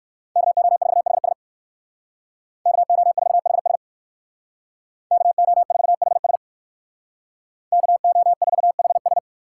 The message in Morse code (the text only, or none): KO4HS KO4HS KO4HS KO4HS